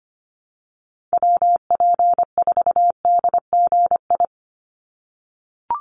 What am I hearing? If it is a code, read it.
Morse code, heard as a keyed tone, about 25 wpm, 700 Hz. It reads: WP4DGI
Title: USA Call Signs 25wpm